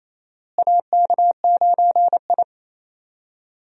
Morse code, heard as a keyed tone, about 28 wpm, 700 Hz. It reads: AK9I
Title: USA Call Signs - 4 Characters - Repeated 3x - 28wpm